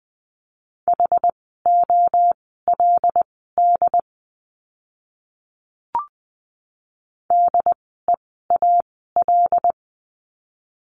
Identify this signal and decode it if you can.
Morse code, heard as a keyed tone, about 20 wpm, spaced as for 14 wpm, 700 Hz. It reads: HOLD DEAL
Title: 075 Lesson 13f - Letter L - Words 20wpm